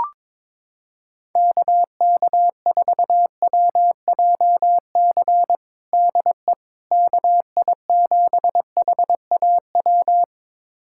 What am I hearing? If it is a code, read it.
Morse code, heard as a keyed tone, about 21 wpm, 700 Hz. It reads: KK4WJC DE KI7HAW
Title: USA Callsign DE Callsign - 6 Characters - 22wpm